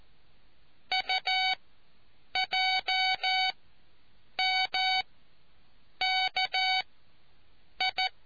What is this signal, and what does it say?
Morse code, heard as a keyed tone, about 14 wpm, spaced as for 7 wpm, 750 Hz. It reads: UJMKI